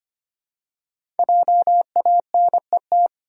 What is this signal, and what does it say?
Morse code, heard as a keyed tone, about 25 wpm, 700 Hz. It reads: JANET
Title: USA Common Names - 25wpm